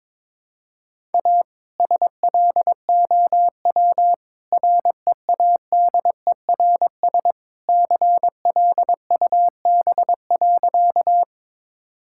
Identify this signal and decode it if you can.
Morse code, heard as a keyed tone, about 22 wpm, 690 Hz. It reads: A SLOW READERS CLUB.
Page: morse-code-ninja-practice-22wpm